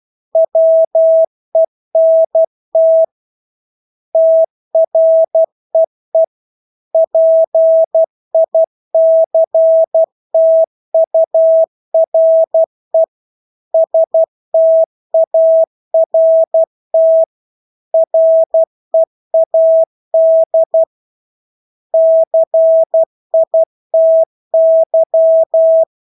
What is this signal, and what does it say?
Morse code, heard as a keyed tone, about 12 wpm, 650 Hz. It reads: WENT TREE PICTURE START READ CITY
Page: Engelske ord 12 wpm